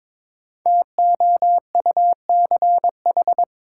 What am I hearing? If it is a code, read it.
Morse code, heard as a keyed tone, about 22 wpm, 700 Hz. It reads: TOUCH
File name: 5 Letter Words - First 3 Letters Given - 22wpm